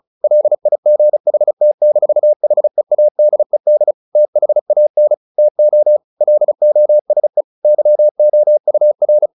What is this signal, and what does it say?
Morse code, heard as a keyed tone, about 34 wpm, 600 Hz. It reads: LIGHT-HEADED THAN TO LOSE YOUR
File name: the old man and the sea 2 35wpm